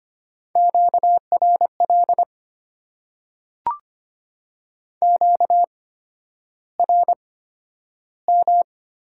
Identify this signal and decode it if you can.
Morse code, heard as a keyed tone, about 25 wpm, spaced as for 11 wpm, 700 Hz. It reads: QRL QRM
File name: Top 90 QSO Elements - 8x Character Spacing - 25wpm